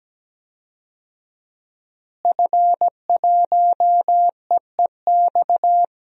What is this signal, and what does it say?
Morse code, heard as a keyed tone, about 17 wpm, 700 Hz. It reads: F1EEX
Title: DXCC-100 Call Signs - 5 Characters - Repeated 2x - 17wpm